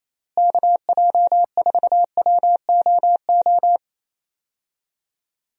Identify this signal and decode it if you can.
Morse code, heard as a keyed tone, about 28 wpm, 700 Hz. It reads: KJ4WOO
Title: USA Call Signs - 6 Characters - Repeated 3x - 28wpm